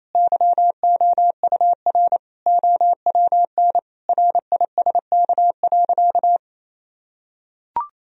Morse code, heard as a keyed tone, about 27 wpm, 700 Hz. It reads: YOUR OWN RISK.
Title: Sentences from Top 1000 Words - Review Part 4 - 28wpm